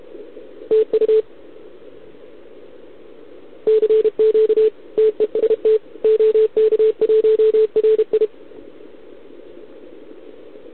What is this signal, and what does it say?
Morse code, heard as a keyed tone, about 31 wpm, 440 Hz. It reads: TU CQ TEST OK1RI
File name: ARCHIV / OKDX2020 / OK1RIb